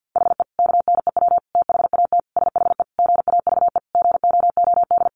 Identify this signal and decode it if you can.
Morse code, heard as a keyed tone, about 50 wpm, 700 Hz. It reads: HE KNEW THAT SHE GAVE GOOD
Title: Sentences from Top 1100 Words 50wpm